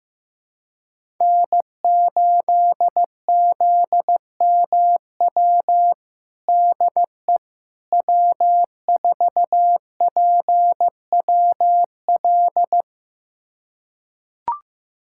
Morse code, heard as a keyed tone, about 15 wpm, 700 Hz. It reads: N8ZMW DE W4PWL
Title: USA Callsign DE Callsign - 5 Characters - 15wpm